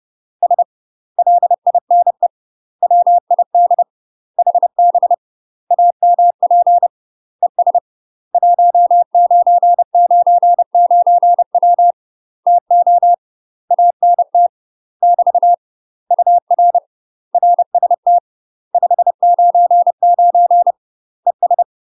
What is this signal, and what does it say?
Morse code, heard as a keyed tone, about 30 wpm, 700 Hz. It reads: S LINE WID HB AMP ES 1999W TO ANT <BT> UR RST 599 ES
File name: UK MASTER QSOs TRAINING / 30WPM / UK MASTER QSOs 0020